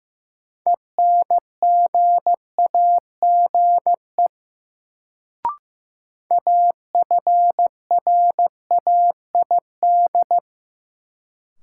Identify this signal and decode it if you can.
Morse code, heard as a keyed tone, about 15 wpm, 700 Hz. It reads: ENGAGE AFRAID
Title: Sentences from Top 1500 Words - New Word Drill Alt - 15wpm